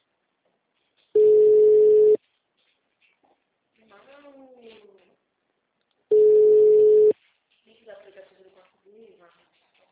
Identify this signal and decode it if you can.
Ringback tone (Europe, 425 Hz), with speech.
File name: out- -2051- - - .23485